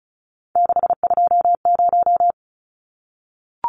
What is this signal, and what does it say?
Morse code, heard as a keyed tone, about 34 wpm, 700 Hz. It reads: 620
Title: Numbers - 3-Digit 35wpm